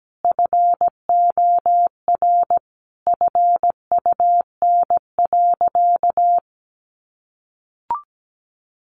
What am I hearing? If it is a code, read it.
Morse code, heard as a keyed tone, about 17 wpm, 700 Hz. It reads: FOR FUN.
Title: Easy Phrases 17wpm